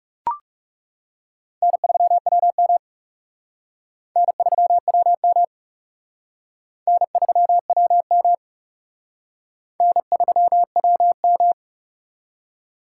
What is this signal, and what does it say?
Morse code, heard as a keyed tone, about 36 wpm, 700 Hz. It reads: N3WM N3WM N3WM N3WM
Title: Speed Racing - USA 1x2 Call Signs 30wpm